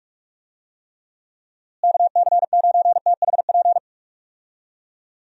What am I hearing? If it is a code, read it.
Morse code, heard as a keyed tone, about 46 wpm, 700 Hz. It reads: KC9THP
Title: USA Call Signs - 6 Characters - Repeated 2x - 45wpm